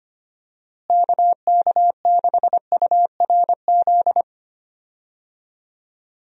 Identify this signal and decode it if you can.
Morse code, heard as a keyed tone, about 25 wpm, 700 Hz. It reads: KK6URZ
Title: USA Call Signs - 6 Characters - Repeated 3x - 25wpm